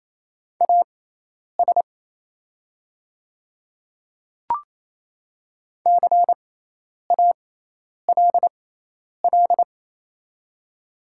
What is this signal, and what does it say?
Morse code, heard as a keyed tone, about 28 wpm, spaced as for 10 wpm, 690 Hz. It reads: AS CALL
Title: 6x Character Spacing - Top 200 Words - 28wpm